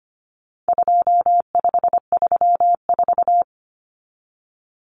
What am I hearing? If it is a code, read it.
Morse code, heard as a keyed tone, about 25 wpm, 700 Hz. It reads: 2534
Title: Numbers - 4-Digit 25wpm